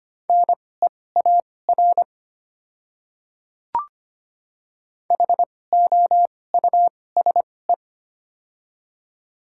Morse code, heard as a keyed tone, about 25 wpm, spaced as for 18 wpm, 700 Hz. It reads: NEAR HOUSE
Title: 2x Character Spacing - Top 200 Words - 25wpm